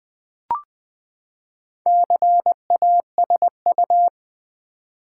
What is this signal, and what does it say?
Morse code, heard as a keyed tone, about 20 wpm, 700 Hz. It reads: CASU
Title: Build A Word - 4 Letters At A Time - 8-Letter Words - 20wpm